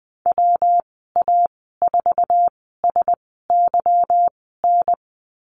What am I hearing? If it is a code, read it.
Morse code, heard as a keyed tone, about 20 wpm, spaced as for 15 wpm, 700 Hz. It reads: WA4SYN
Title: 154 Lesson 25g - Letter p - Call Signs 20wpm